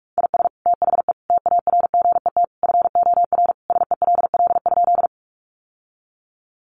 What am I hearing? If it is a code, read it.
Morse code, heard as a keyed tone, about 44 wpm, 700 Hz. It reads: IS THE TARGET FOR SELL?
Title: Sentences from Top 1200 Words 45wpm